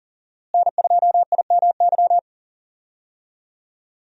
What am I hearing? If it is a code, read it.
Morse code, heard as a keyed tone, about 40 wpm, 700 Hz. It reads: N2IMY
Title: USA Call Signs - 5 Characters - Repeated 2x - 40wpm